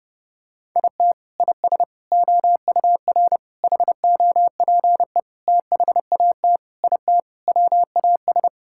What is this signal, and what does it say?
Morse code, heard as a keyed tone, about 30 wpm, 700 Hz. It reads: IT IS OUR HOPE THAT IT WAS
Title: Sentences from Top 1000 Words - Review Part 5 - 30wpm